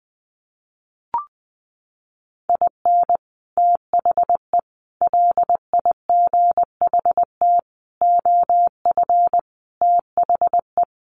Morse code, heard as a keyed tone, about 20 wpm, 700 Hz. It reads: IN THE LIGHT OF THE
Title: Sets of 5 Words Encore2 20wpm